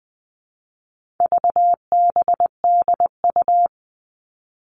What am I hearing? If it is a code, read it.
Morse code, heard as a keyed tone, about 20 wpm, 700 Hz. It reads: VBDU
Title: All Cut Numbers - 4-Digit 20wpm